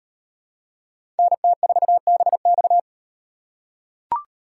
Morse code, heard as a keyed tone, about 36 wpm, 700 Hz. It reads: NT4BX